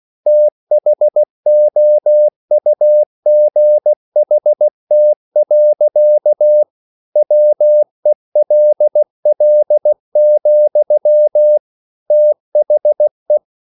Morse code, heard as a keyed tone, about 16 wpm, 600 Hz. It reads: THOUGHT. WELL, THE